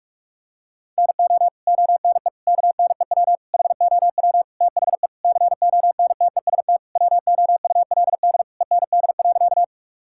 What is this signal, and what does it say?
Morse code, heard as a keyed tone, about 41 wpm, 700 Hz. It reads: NO ONE KNEW HOW THE CONTEST WOULD END.
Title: Sentences from Top 1600 Words 45wpm